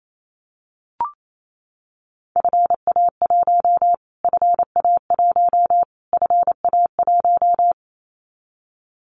Morse code, heard as a keyed tone, about 28 wpm, 700 Hz. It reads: FA1 FA1 FA1
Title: Callsign Prefixes for Top 25 DXCC - Repeated 3x - 28wpm